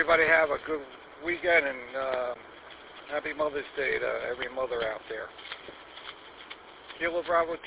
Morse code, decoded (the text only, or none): none